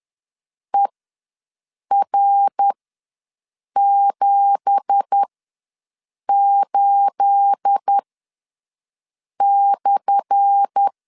Morse code, heard as a keyed tone, about 10 wpm, spaced as for 6 wpm, 800 Hz. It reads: ER78/
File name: LEZIONI / 10 SIGNS / 10A SLASH ws 10wpm 4121